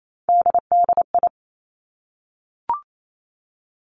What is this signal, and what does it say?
Morse code, heard as a keyed tone, about 28 wpm, 700 Hz. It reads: DDI
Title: Three Challenging Letter-Number Combinations 28wpm